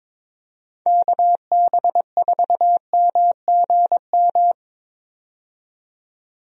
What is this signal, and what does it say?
Morse code, heard as a keyed tone, about 21 wpm, 700 Hz. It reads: KB4MGM
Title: USA Call Signs - 6 Characters - 22wpm